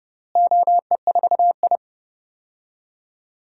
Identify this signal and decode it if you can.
Morse code, heard as a keyed tone, about 32 wpm, 700 Hz. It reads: OE4I
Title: DXCC-100 Call Signs - 4 Characters - Repeated 3x - 30wpm